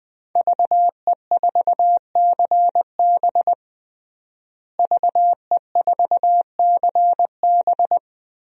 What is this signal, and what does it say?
Morse code, heard as a keyed tone, about 20 wpm, 700 Hz. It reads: VE4CB VE4CB
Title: DXCC-100 Call Signs - 5 Characters - Repeated 3x - 20wpm